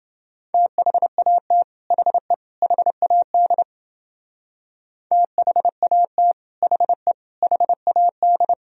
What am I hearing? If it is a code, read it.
Morse code, heard as a keyed tone, about 28 wpm, 700 Hz. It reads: THAT HE HAD THAT HE HAD